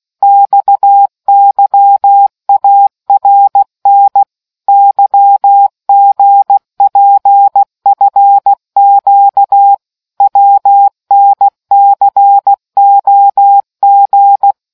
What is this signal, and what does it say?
Morse code, heard as a keyed tone, about 16 wpm, 800 Hz. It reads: XYARN YGPFQ WNCOG